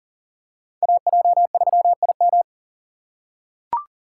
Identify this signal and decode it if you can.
Morse code, heard as a keyed tone, about 39 wpm, 700 Hz. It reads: AJ3IM
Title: USA Call Signs - 5 Characters - 4x Character Spacing - 40wpm